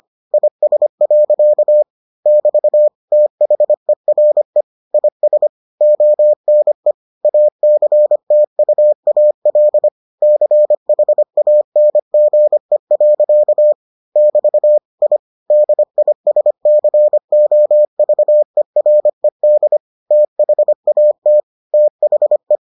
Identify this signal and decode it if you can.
Morse code, heard as a keyed tone, about 25 wpm, 600 Hz. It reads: IS. <BT> THERE IS ONE ACTUAL CHANGE. <BT> I DISCOVERED THAT THE